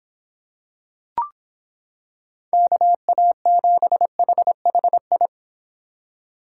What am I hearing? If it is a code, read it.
Morse code, heard as a keyed tone, about 26 wpm, 700 Hz. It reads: KA7HHI